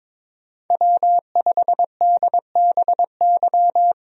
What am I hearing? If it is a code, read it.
Morse code, heard as a keyed tone, about 22 wpm, 700 Hz. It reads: W5DBY
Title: USA Call Signs - 5 Characters - Repeated 2x - 22wpm